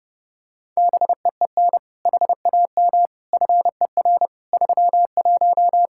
Morse code, heard as a keyed tone, about 30 wpm, 700 Hz. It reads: BEEN HAM FER 31
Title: Practice QSO - Exchange 3 - More Info - 30wpm